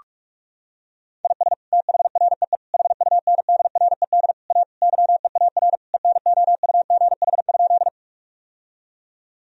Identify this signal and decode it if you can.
Morse code, heard as a keyed tone, about 43 wpm, 700 Hz. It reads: IS THREE HUNDRED A YEAR ENOUGH?
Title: Sentences from Top 1000 Words - Review Part 2 - 45wpm